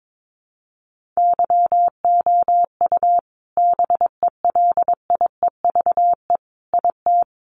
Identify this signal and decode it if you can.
Morse code, heard as a keyed tone, about 22 wpm, 700 Hz. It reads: YOU BELIEVE IT
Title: Sentences from Top 1400 Words 22wpm